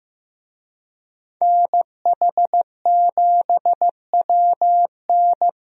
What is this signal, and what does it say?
Morse code, heard as a keyed tone, about 15 wpm, 700 Hz. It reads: NH7WN
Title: USA Call Signs - 5 Characters - Repeated 2x - 15wpm